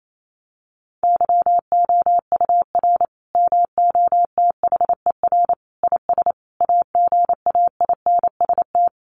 Morse code, heard as a keyed tone, about 28 wpm, 700 Hz. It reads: YOUR MOTHER IS AGAINST
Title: Sentences from Top 1400 Words 28wpm